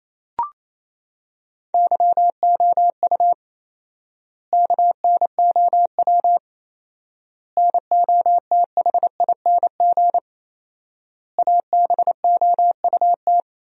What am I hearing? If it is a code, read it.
Morse code, heard as a keyed tone, about 28 wpm, 700 Hz. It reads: YOU KNOW NOTHING ABOUT
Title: 4x Word-Spacing - Sentences from Top 500 Words - 28wpm